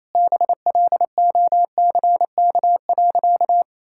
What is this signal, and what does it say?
Morse code, heard as a keyed tone, about 28 wpm, 700 Hz. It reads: BLOCK.
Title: Sentences from Top 1600 Words 28wpm